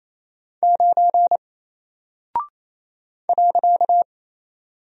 Morse code, heard as a keyed tone, about 27 wpm, 700 Hz. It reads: 9 .
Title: Single Character (Letter:Number:Punctuation:Prosign) 28wpm